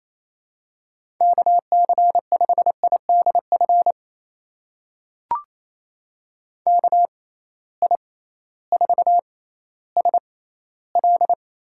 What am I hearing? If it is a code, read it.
Morse code, heard as a keyed tone, about 27 wpm, spaced as for 15 wpm, 700 Hz. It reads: KC5IDF KI4SL
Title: USA Call Signs - 6 Characters - 6x Character Spacing - 28wpm